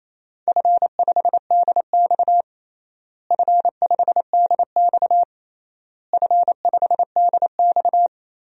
Morse code, heard as a keyed tone, about 27 wpm, 700 Hz. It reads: F5DX F5DX F5DX
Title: DXCC-100 Call Signs - 4 Characters - Repeated 3x - 28wpm